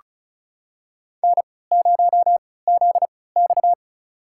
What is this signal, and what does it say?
Morse code, heard as a keyed tone, about 36 wpm, spaced as for 20 wpm, 700 Hz. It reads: N0ZX
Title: USA Call Signs - 4 Characters - 3x Character Spacing - 35wpm